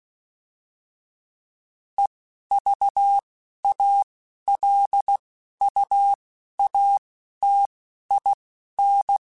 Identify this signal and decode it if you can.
Morse code, heard as a keyed tone, about 16 wpm, spaced as for 11 wpm, 800 Hz. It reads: EVALUATIN